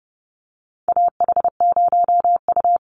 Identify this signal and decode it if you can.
Morse code, heard as a keyed tone, about 30 wpm, 700 Hz. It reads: AH0U
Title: Call Signs - 30wpm